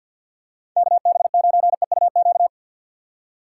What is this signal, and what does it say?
Morse code, heard as a keyed tone, about 53 wpm, 700 Hz. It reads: KB9EUQ
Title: USA Call Signs - 6 Characters - 6x Character Spacing - 50wpm